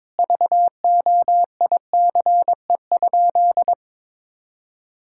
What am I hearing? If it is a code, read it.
Morse code, heard as a keyed tone, about 22 wpm, 700 Hz. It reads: VOICE?
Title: Sentences from Top 1600 Words 22wpm